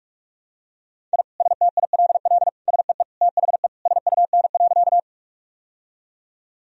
Morse code, heard as a keyed tone, about 41 wpm, 700 Hz. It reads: I STILL SEE THE SUN.
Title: Sentences from Top 1000 Words - Review Part 1 - 45wpm